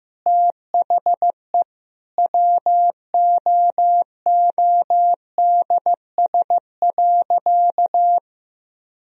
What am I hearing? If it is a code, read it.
Morse code, heard as a keyed tone, about 15 wpm, 700 Hz. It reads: THE WOODS.
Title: Sentences from Top 1000 Words - Review Part 2 - 15wpm